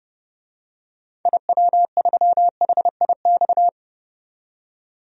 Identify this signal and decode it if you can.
Morse code, heard as a keyed tone, about 30 wpm, 700 Hz. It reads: IW3HIX
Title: DXCC-100 Call Signs - 6 Characters - 30wpm